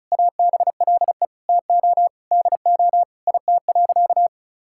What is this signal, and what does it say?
Morse code, heard as a keyed tone, about 34 wpm, 700 Hz. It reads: ABLE TO DO IT.
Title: Sentences from Top 1000 Words - Review Part 6 - 35wpm